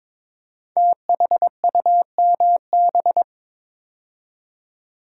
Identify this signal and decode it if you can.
Morse code, heard as a keyed tone, about 22 wpm, 700 Hz. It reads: THUMB
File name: 5 Letter Words - First 3 Letters Given - 22wpm